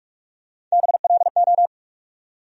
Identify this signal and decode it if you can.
Morse code, heard as a keyed tone, about 41 wpm, 700 Hz. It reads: BLO
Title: USA Call Sign Postfixes - 3 Letters 45wpm